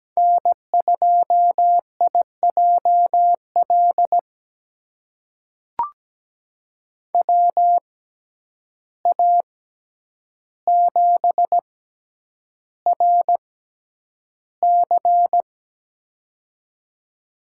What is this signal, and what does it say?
Morse code, heard as a keyed tone, about 17 wpm, spaced as for 9 wpm, 700 Hz. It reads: N2IJL WA7RC
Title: USA Call Signs - 5 Characters - 6x Character Spacing - 17wpm